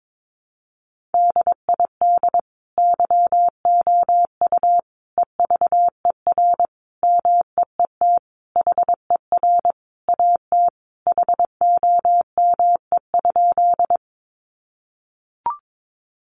Morse code, heard as a keyed tone, about 22 wpm, 700 Hz. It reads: DID YOU EVER MEET HER AT HOME?
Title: Sentences from Top 1100 Words 22wpm